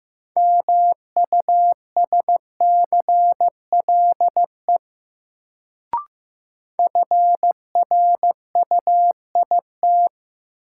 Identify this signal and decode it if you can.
Morse code, heard as a keyed tone, about 15 wpm, 700 Hz. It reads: MUSCLE FRUIT
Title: Sentences from Top 1300 Words - New Word Drill - 15wpm